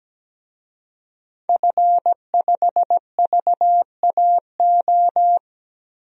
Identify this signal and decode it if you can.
Morse code, heard as a keyed tone, about 17 wpm, 700 Hz. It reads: F5VAO
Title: DXCC-100 Call Signs - 5 Characters - Repeated 3x - 17wpm